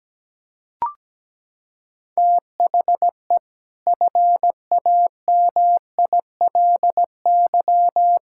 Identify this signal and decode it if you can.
Morse code, heard as a keyed tone, about 17 wpm, 700 Hz. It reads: THE FAMILY
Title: Sentences from Top 1600 Words 17wpm